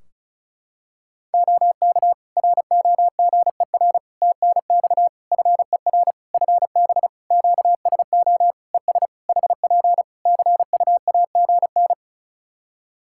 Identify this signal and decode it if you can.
Morse code, heard as a keyed tone, about 34 wpm, 700 Hz. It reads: OK ROGER TNX FER FB QSO ES HP CUAGN
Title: Practice QSO - Exchange 4 - Ending - 35wpm